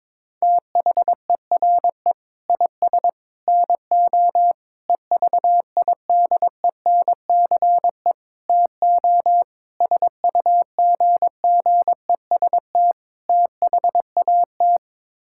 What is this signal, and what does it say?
Morse code, heard as a keyed tone, about 22 wpm, 700 Hz. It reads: THERE IS NO EVIDENCE TO SUGGEST THAT
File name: practice / 22wpm / Sets of 7 Words - 22wpm